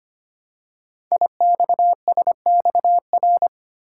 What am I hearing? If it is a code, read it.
Morse code, heard as a keyed tone, about 25 wpm, 700 Hz. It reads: IXSXR
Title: Code Groups - 5 Letter:Numbers - 25wpm